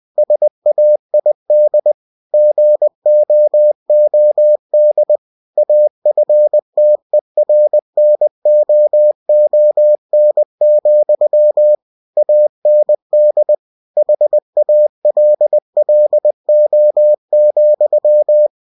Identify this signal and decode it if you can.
Morse code, heard as a keyed tone, about 20 wpm, 600 Hz. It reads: SAID GOOD AFTERNOON, AND HALLO,
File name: cw-books / winnie-the-pooh / winnie-20wpm-page-0066